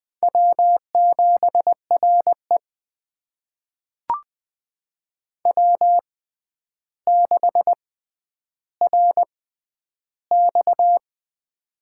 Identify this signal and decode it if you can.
Morse code, heard as a keyed tone, about 20 wpm, spaced as for 10 wpm, 700 Hz. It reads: W7RE W6RX
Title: USA Call Signs - 4 Characters - 6x Character Spacing - 20wpm